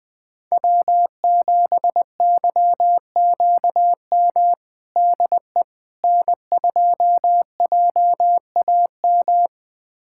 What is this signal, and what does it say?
Morse code, heard as a keyed tone, about 21 wpm, 700 Hz. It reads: W7YQM DE N2JAM
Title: USA Callsign DE Callsign - 5 Characters - 20wpm